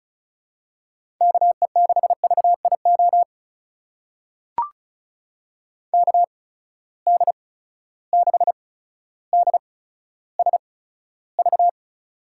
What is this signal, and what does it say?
Morse code, heard as a keyed tone, about 34 wpm, spaced as for 15 wpm, 700 Hz. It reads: KE6VIO KD6DSV
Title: USA Call Signs - 6 Characters - 8x Character Spacing - 35wpm